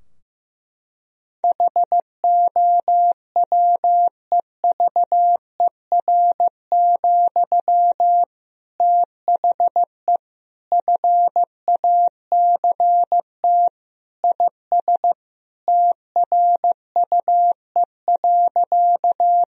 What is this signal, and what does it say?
Morse code, heard as a keyed tone, about 15 wpm, 700 Hz. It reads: HOWEVER, THE FACT IS TRUE.
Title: Sentences from Top 1000 Words - Review Part 3 - 15wpm